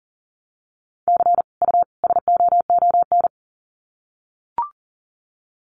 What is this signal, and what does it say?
Morse code, heard as a keyed tone, about 40 wpm, 700 Hz. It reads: C U SOON
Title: Common QSO Phrases 40wpm